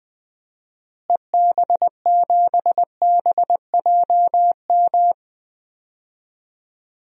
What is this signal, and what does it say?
Morse code, heard as a keyed tone, about 20 wpm, 700 Hz. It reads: EB7BJM
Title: DXCC-100 Call Signs - 6 Characters - Repeated 3x - 20wpm